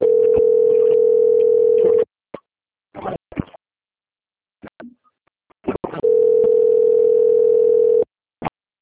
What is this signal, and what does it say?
Ringback tone (North America), 440 + 480 Hz.
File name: out- - - - - .8332